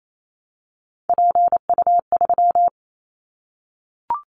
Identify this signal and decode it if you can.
Morse code, heard as a keyed tone, about 28 wpm, 700 Hz. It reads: PU3
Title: Callsign Prefixes for Top 100 DXCC - 28wpm